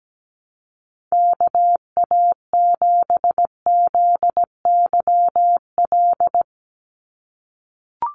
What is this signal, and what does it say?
Morse code, heard as a keyed tone, about 17 wpm, 700 Hz. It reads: KA7ZYL
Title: USA Call Signs 17wpm